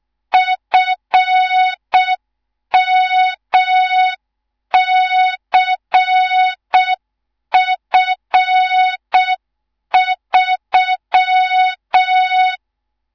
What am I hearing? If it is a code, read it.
Morse code, heard as a keyed tone, about 6 wpm, 3700 Hz. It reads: FMCF3
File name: Morze-HA2MN / hang / 18 IRASJELEK2